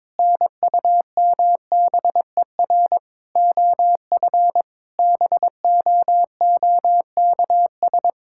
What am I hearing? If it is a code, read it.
Morse code, heard as a keyed tone, about 22 wpm, 700 Hz. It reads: NUMBER OF BOOKS
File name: practice / 22wpm / Sentences with 8-Word Phrases - 22wpm